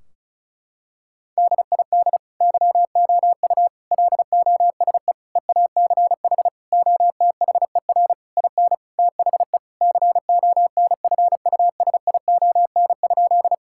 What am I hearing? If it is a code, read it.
Morse code, heard as a keyed tone, about 34 wpm, 700 Hz. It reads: DID YOU LOSE EACH OTHER IN THE CONFUSION?
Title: Sentences from Top 1600 Words 35wpm